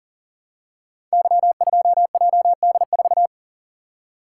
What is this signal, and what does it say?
Morse code, heard as a keyed tone, about 38 wpm, 700 Hz. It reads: Y2JD4
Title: Code Groups - 5 Letter:Numbers - 40wpm